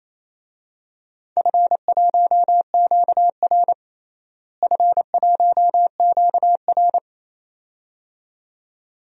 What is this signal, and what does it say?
Morse code, heard as a keyed tone, about 28 wpm, 700 Hz. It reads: F1QR F1QR
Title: DXCC-100 Call Signs - 4 Characters - Repeated 2x - 28wpm